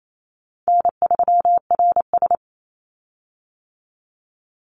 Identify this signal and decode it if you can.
Morse code, heard as a keyed tone, about 28 wpm, 700 Hz. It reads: N3RS
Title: Easy Call Signs - 28wpm